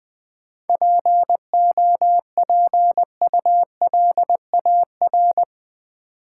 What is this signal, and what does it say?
Morse code, heard as a keyed tone, about 20 wpm, 700 Hz. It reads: POPULAR
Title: Sentences from Top 1000 Words - Review Part 3 - New Word Drill - 20wpm